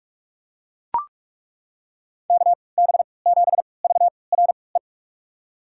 Morse code, heard as a keyed tone, about 42 wpm, spaced as for 25 wpm, 690 Hz. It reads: KB7VRE
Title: USA Call Signs - 6 Characters - 3x Character Spacing - 45wpm